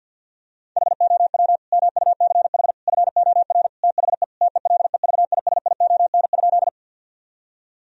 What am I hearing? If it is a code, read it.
Morse code, heard as a keyed tone, about 52 wpm, 700 Hz. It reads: HOW MUCH FOR THE TELEVISION?
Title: Sentences from Top 1000 Words - Review Part 4 - 50wpm